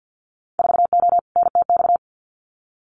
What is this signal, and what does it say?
Morse code, heard as a keyed tone, about 50 wpm, 700 Hz. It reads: 4O NTX
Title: Winter Field Day Exchanges 50wpm